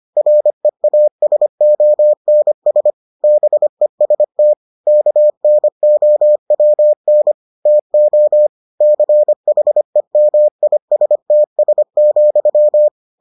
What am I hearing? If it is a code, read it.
Morse code, heard as a keyed tone, about 25 wpm, 600 Hz. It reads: REASONS BEST KNOWN TO CHEMISTS,